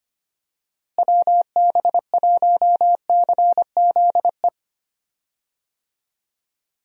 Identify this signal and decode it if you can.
Morse code, heard as a keyed tone, about 25 wpm, 700 Hz. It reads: WB1CZE
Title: USA Call Signs - 6 Characters - Repeated 3x - 25wpm